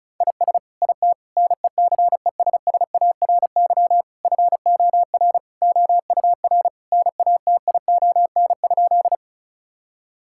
Morse code, heard as a keyed tone, about 36 wpm, 700 Hz. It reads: IS IT NECESSARY FOR OUR NATION?
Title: Sentences from Top 1000 Words - Review Part 6 - 35wpm